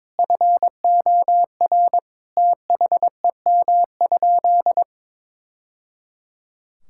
Morse code, heard as a keyed tone, about 22 wpm, 700 Hz. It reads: FOR THEM?
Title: Sentences from Top 1000 Words - Review Part 5 - 22wpm